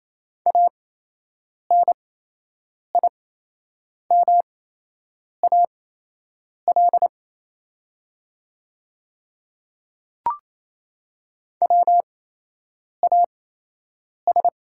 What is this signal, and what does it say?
Morse code, heard as a keyed tone, about 27 wpm, spaced as for 8 wpm, 690 Hz. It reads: ANIMAL WAS
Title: 8x Character Spacing - Top 200 Words - 28wpm